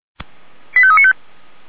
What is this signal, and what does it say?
Five-tone selective call (ZVEI), heard as quick runs of five tones, tones of about 70 ms. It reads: 85395
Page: SDR/ZVEI